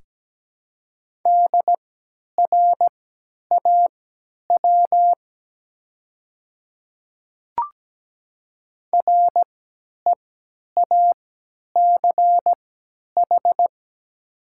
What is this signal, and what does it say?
Morse code, heard as a keyed tone, about 17 wpm, spaced as for 10 wpm, 700 Hz. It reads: DRAW REACH
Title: 3x Character Spacing - Top 500 Words - 17wpm